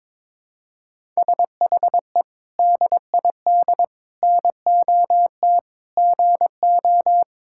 Morse code, heard as a keyed tone, about 22 wpm, 700 Hz. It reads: SHE DID NOT GO
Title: Sentences from Top 1000 Words - Review Part 6 - 22wpm